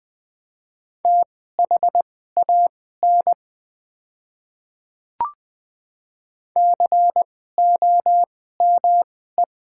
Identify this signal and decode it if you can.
Morse code, heard as a keyed tone, about 20 wpm, spaced as for 15 wpm, 700 Hz. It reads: THAN COME
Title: 2x Character Spacing - Top 100 Words - 20wpm